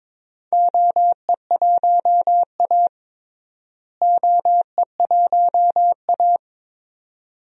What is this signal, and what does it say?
Morse code, heard as a keyed tone, about 22 wpm, 700 Hz. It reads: OE1A OE1A
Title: DXCC-100 Call Signs - 4 Characters - Repeated 2x - 22wpm